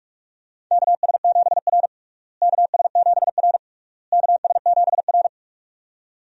Morse code, heard as a keyed tone, about 45 wpm, 700 Hz. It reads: KS7R KS7R KS7R
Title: USA Call Signs - 4 Characters - Repeated 3x - 45wpm